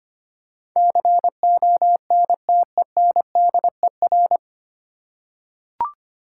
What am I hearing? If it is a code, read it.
Morse code, heard as a keyed tone, about 25 wpm, 700 Hz. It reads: CONTENDER
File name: Nine Letter Words - 25wpm